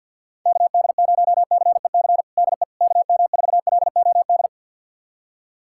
Morse code, heard as a keyed tone, about 52 wpm, 700 Hz. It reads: KD0CEC DE KM4LOD
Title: USA Callsign DE Callsign - 6 Characters - 50wpm